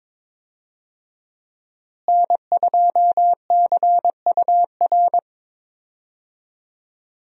Morse code, heard as a keyed tone, about 22 wpm, 700 Hz. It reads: N2CUR
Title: USA Call Signs - 5 Characters - Repeated 3x - 22wpm